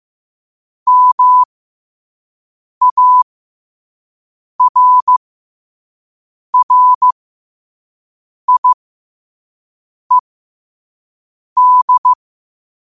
Morse code, heard as a keyed tone, about 15 wpm, spaced as for 5 wpm, 1000 Hz. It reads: MARRIED
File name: qotdfiles / 5wpm